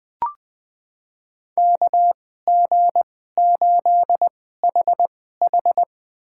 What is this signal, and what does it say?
Morse code, heard as a keyed tone, about 20 wpm, spaced as for 14 wpm, 700 Hz. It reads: KG8HH
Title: USA Call Signs - 2x Character Spacing - 20wpm